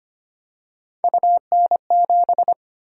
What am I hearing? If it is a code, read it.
Morse code, heard as a keyed tone, about 25 wpm, 700 Hz. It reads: UN7
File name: Three Letter-Number Combinations - 25wpm